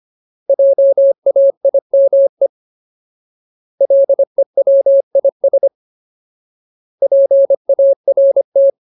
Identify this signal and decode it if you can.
Morse code, heard as a keyed tone, about 25 wpm, 550 Hz. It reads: JAIME LEWIS PART
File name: Ultimate Word List / 25WPM Ultimate